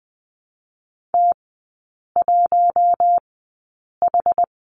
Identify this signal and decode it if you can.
Morse code, heard as a keyed tone, about 20 wpm, spaced as for 8 wpm, 690 Hz. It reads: T1H
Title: 060 Lesson 11e - Letter h - Three Characters 20wpm